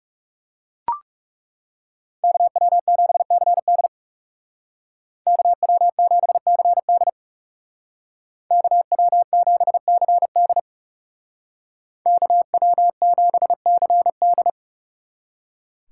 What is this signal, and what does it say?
Morse code, heard as a keyed tone, about 40 wpm, 700 Hz. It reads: KW7CD KW7CD KW7CD KW7CD